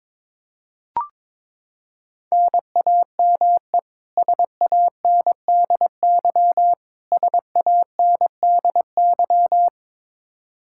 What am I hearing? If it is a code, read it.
Morse code, heard as a keyed tone, about 22 wpm, 700 Hz. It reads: NAME SANDY SANDY
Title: K1USN SST QSO Practice 22wpm